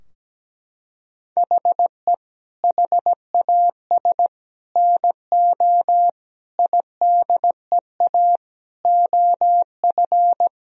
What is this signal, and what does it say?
Morse code, heard as a keyed tone, about 17 wpm, 700 Hz. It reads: HE HAS NO IDEA OF
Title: Sentences from Top 1000 Words - Review Part 3 - 17wpm